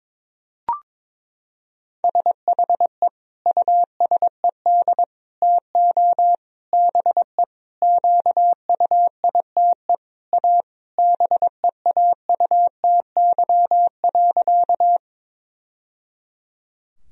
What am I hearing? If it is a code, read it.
Morse code, heard as a keyed tone, about 22 wpm, 700 Hz. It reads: SHE USED TO BE QUITE A BEAUTY.
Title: Sentences from Top 1000 Words - Review Part 2 - 22wpm